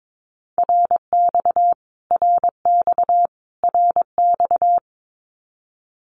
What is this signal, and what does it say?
Morse code, heard as a keyed tone, about 22 wpm, 700 Hz. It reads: RX RX RX
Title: Abbreviations Repeated 3x - 22wpm